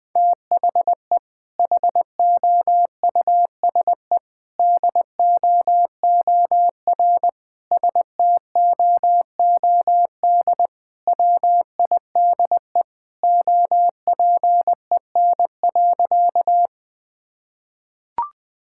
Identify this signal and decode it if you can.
Morse code, heard as a keyed tone, about 20 wpm, 700 Hz. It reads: THE HOUSE DOOR STOOD WIDE OPEN.
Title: Sentences from Top 1000 Words - Review Part 6 - 20wpm